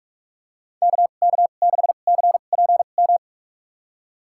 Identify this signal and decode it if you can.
Morse code, heard as a keyed tone, about 45 wpm, spaced as for 32 wpm, 700 Hz. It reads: KK6CPM
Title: USA Call Signs - 6 Characters - 2x Character Spacing - 45wpm